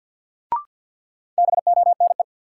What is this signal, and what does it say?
Morse code, heard as a keyed tone, about 53 wpm, 700 Hz. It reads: BONE
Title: Sentences from Top 1600 Words - New Word Drill Alt - 50wpm